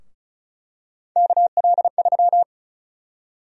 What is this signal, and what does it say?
Morse code, heard as a keyed tone, about 34 wpm, 700 Hz. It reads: KL3
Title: USA Call Sign Prefixes - 35wpm